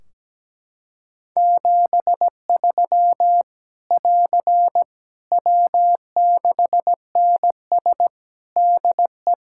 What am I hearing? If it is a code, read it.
Morse code, heard as a keyed tone, about 17 wpm, 700 Hz. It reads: 73 <AR> W6NS DE
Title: Practice QSO - Exchange 4 - Ending - 17wpm